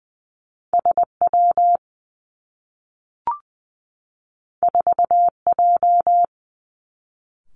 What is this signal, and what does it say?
Morse code, heard as a keyed tone, about 19 wpm, 700 Hz. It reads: S W 4 J
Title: Two Challenging Letter-Number Combinations 20wpm